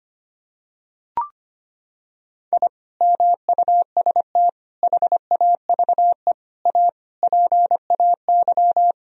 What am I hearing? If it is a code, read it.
Morse code, heard as a keyed tone, about 25 wpm, 700 Hz. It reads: I MUST HAVE A PAY
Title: Sentences from Top 1000 Words - Review Part 4 - 25wpm